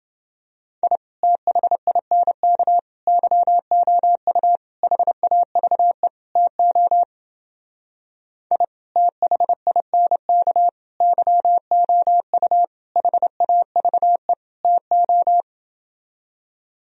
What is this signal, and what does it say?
Morse code, heard as a keyed tone, about 28 wpm, 700 Hz. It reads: I THINK YOU HAVE TO I THINK YOU HAVE TO